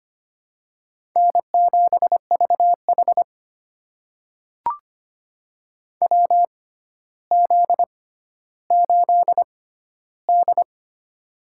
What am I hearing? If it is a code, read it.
Morse code, heard as a keyed tone, about 25 wpm, spaced as for 13 wpm, 700 Hz. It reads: N7VH WZ8D
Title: USA Call Signs - 4 Characters - 6x Character Spacing - 25wpm